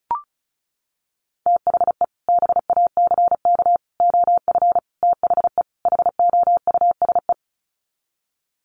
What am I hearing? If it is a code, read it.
Morse code, heard as a keyed tone, about 34 wpm, 700 Hz. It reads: THE BACK OF THE HOUSE
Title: Sets of 5 Words Encore2 35wpm